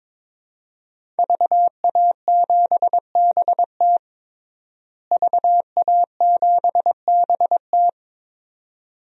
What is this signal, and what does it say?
Morse code, heard as a keyed tone, about 21 wpm, 700 Hz. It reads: VA7BT VA7BT
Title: DXCC-100 Call Signs - 5 Characters - Repeated 2x - 22wpm